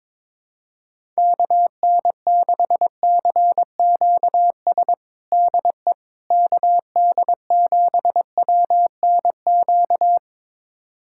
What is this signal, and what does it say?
Morse code, heard as a keyed tone, about 22 wpm, 700 Hz. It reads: KN6CQS DE KD7WNQ
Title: USA Callsign DE Callsign - 6 Characters - 22wpm